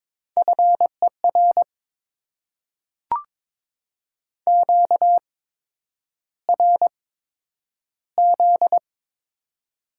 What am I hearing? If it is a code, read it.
Morse code, heard as a keyed tone, about 22 wpm, spaced as for 9 wpm, 700 Hz. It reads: FER QRZ
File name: Top 90 QSO Elements - 8x Character Spacing - 22wpm